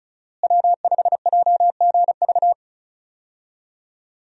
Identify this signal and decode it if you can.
Morse code, heard as a keyed tone, about 34 wpm, 700 Hz. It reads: W5JGV
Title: USA Call Signs - 5 Characters - Repeated 2x - 35wpm